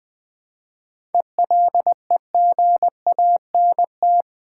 Morse code, heard as a keyed tone, about 20 wpm, 700 Hz. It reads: ELEGANT
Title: 7 Letter Words - First 3 Letters Given - 20wpm